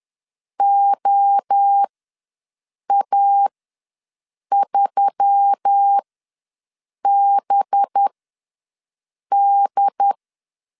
Morse code, heard as a keyed tone, about 10 wpm, spaced as for 6 wpm, 800 Hz. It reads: OA3BD
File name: LEZIONI / 6 PKX0 / 6 P ws 10wpm 3751